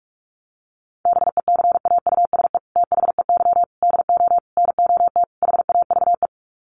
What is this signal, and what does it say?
Morse code, heard as a keyed tone, about 44 wpm, 700 Hz. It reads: BECAUSE THEY DO NOT HAVE
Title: Sets of 5 Words Encore2 45wpm